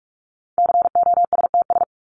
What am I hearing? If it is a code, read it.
Morse code, heard as a keyed tone, about 44 wpm, 700 Hz. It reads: COSTS
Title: ES:S Suffix - Top 100 Words with Base Words 45wpm